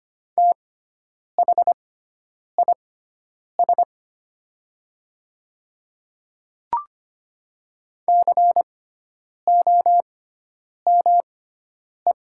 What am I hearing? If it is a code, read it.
Morse code, heard as a keyed tone, about 25 wpm, spaced as for 9 wpm, 700 Hz. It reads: THIS COME
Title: 6x Character Spacing - Top 100 Words - 25wpm